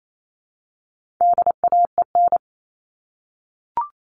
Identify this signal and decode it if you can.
Morse code, heard as a keyed tone, about 28 wpm, 700 Hz. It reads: DAEN
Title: All Cut Numbers - 4-Digit 28wpm